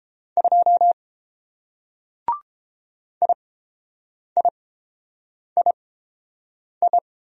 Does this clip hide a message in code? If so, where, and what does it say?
Yes: Morse code from 0.4 to 7.0 s, heard as a keyed tone, about 30 wpm, spaced as for 7 wpm, 700 Hz. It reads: 2 IIII